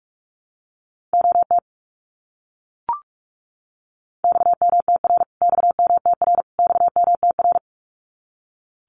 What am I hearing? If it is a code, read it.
Morse code, heard as a keyed tone, about 44 wpm, 700 Hz. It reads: OT XMTR XMTR XMTR
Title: Abbreviations Repeated 3x - 45wpm